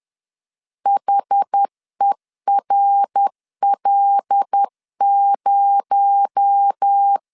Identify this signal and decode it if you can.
Morse code, heard as a keyed tone, about 10 wpm, 800 Hz. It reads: HERL0